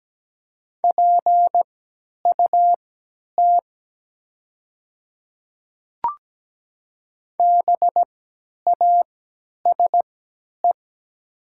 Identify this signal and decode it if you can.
Morse code, heard as a keyed tone, about 17 wpm, spaced as for 10 wpm, 700 Hz. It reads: PUT BASE
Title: 3x Character Spacing - Top 300 Words - 17wpm